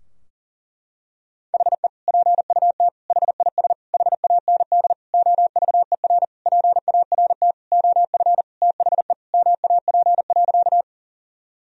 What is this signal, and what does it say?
Morse code, heard as a keyed tone, about 40 wpm, 700 Hz. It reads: HE PUT HIS HAND OVER PART OF THE MAP.
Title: Sentences from Top 1000 Words - Review Part 2 - 40wpm